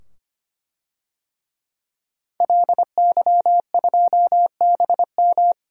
Morse code, heard as a keyed tone, about 25 wpm, 700 Hz. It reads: LY2BM
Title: DXCC-100 Call Signs - 5 Characters - 25wpm